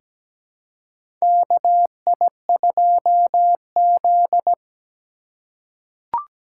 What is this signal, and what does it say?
Morse code, heard as a keyed tone, about 17 wpm, 700 Hz. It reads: KI2Z
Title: USA Call Signs - 4 Characters - 2x Character Spacing - 17wpm